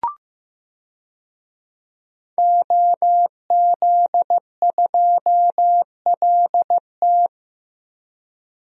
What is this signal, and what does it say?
Morse code, heard as a keyed tone, about 15 wpm, 700 Hz. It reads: OZ2LT
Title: DXCC-100 Call Signs - 5 Characters - Repeated 2x - 15wpm